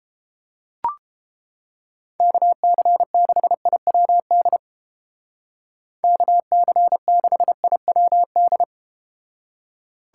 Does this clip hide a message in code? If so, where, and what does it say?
Yes: Morse code from 2.2 to 8.6 s, heard as a keyed tone, about 32 wpm, 700 Hz. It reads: KC6IWD KC6IWD